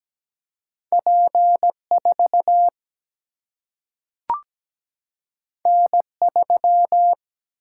Morse code, heard as a keyed tone, about 17 wpm, 700 Hz. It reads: P4 N3
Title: Single-Letter Single-Number 17wpm